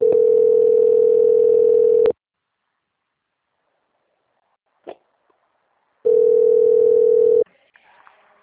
Ringback tone (North America), 440 + 480 Hz.